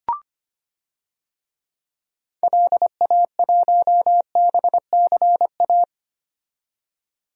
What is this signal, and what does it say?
Morse code, heard as a keyed tone, about 25 wpm, 700 Hz. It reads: LA1BCA
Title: DXCC-100 Call Signs - 6 Characters - 25wpm